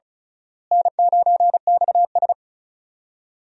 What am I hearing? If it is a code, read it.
Morse code, heard as a keyed tone, about 34 wpm, 700 Hz. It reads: N9XS
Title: USA Call Signs - 4 Characters - Repeated 3x - 35wpm